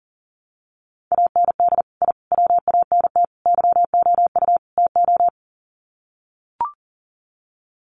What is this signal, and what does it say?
Morse code, heard as a keyed tone, about 40 wpm, 700 Hz. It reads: AND I WANT YOU TO
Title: Sets of 5 Words Encore 40wpm